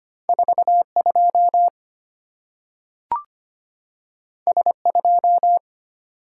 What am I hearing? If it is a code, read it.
Morse code, heard as a keyed tone, about 25 wpm, 690 Hz. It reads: 42 S2
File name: Two Letter-Number Combinations - 25wpm